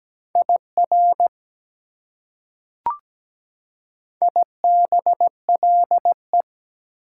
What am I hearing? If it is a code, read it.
Morse code, heard as a keyed tone, about 17 wpm, 700 Hz. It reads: IR IBLE
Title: Prefixes and Suffixes 17wpm